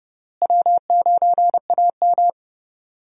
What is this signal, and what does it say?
Morse code, heard as a keyed tone, about 31 wpm, 700 Hz. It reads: W9AM
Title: USA Call Signs - 4 Characters - Repeated 3x - 30wpm